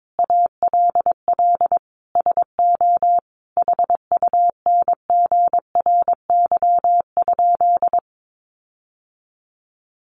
Morse code, heard as a keyed tone, about 22 wpm, 700 Hz. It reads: ALL SO HUNGRY?
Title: Sentences from Top 1400 Words 22wpm